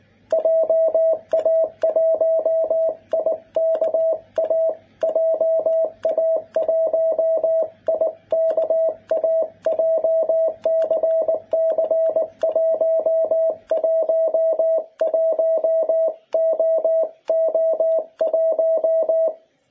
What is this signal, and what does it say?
Morse code, heard as a keyed tone, about 20 wpm, 650 Hz. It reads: JA1IKA JA1IKA JCC111OO1
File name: jh8cbh / 2021cw